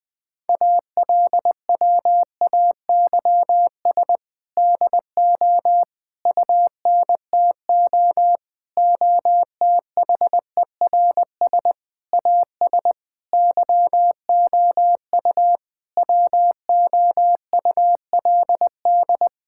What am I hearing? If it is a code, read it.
Morse code, heard as a keyed tone, about 20 wpm, 700 Hz. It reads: ALWAYS DO UNTO OTHERS AS YOU WOULD
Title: Sentences with 8-Word Phrases - 20wpm